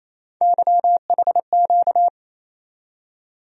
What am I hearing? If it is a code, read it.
Morse code, heard as a keyed tone, about 28 wpm, 700 Hz. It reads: YHQ
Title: USA Call Sign Postfixes - 3 Letters 28wpm